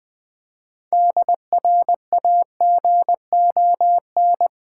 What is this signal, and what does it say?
Morse code, heard as a keyed tone, about 19 wpm, 700 Hz. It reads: DRAGON
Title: Difficult Words with Hints - 20wpm